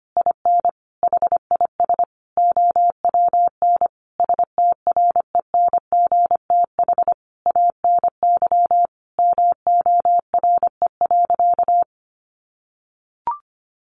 Morse code, heard as a keyed tone, about 25 wpm, 700 Hz. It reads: IN HIS OWN STRENGTH ANY MORE.
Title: Sentences from Top 1300 Words 25wpm